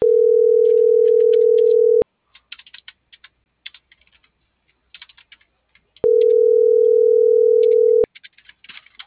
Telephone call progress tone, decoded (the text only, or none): ringback tone